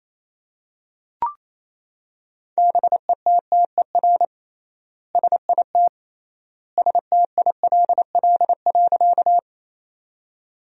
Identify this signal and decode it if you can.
Morse code, heard as a keyed tone, about 28 wpm, 700 Hz. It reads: BETTER SIT STILL.
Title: 3x Word-Spacing - Sentences from Top 500 Words - 28wpm